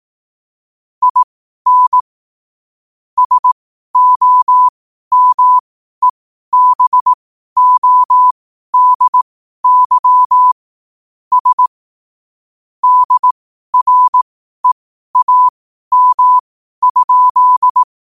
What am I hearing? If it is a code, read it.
Morse code, heard as a keyed tone, about 18 wpm, spaced as for 12 wpm, 1000 Hz. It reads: IN SOMEBODY S DREAM?
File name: qotdfiles / 12wpm